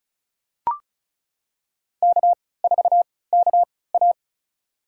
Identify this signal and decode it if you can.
Morse code, heard as a keyed tone, about 36 wpm, spaced as for 20 wpm, 700 Hz. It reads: K4KA
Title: USA Call Signs - 4 Characters - 3x Character Spacing - 35wpm